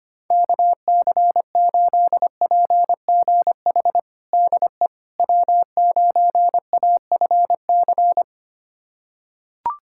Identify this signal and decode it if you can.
Morse code, heard as a keyed tone, about 25 wpm, 700 Hz. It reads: KC8PGH DE W9AFC
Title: USA Callsign DE Callsign 25wpm